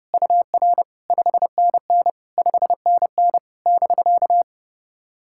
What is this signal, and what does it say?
Morse code, heard as a keyed tone, about 31 wpm, 690 Hz. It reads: UR 5NN 5NN <BK>
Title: SOTA QSO Practice 30wpm